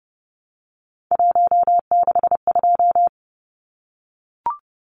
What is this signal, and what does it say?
Morse code, heard as a keyed tone, about 30 wpm, 700 Hz. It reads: 162
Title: Numbers - 3-Digit 30wpm